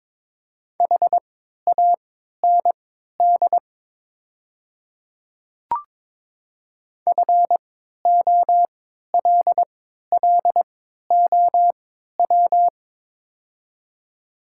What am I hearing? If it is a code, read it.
Morse code, heard as a keyed tone, about 22 wpm, spaced as for 12 wpm, 690 Hz. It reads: HAND FOLLOW
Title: 3x Character Spacing - Top 200 Words - 22wpm